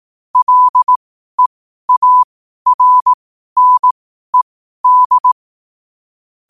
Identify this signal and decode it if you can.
Morse code, heard as a keyed tone, about 18 wpm, spaced as for 12 wpm, 1000 Hz. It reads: LEARNED
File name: qotdfiles / 12wpm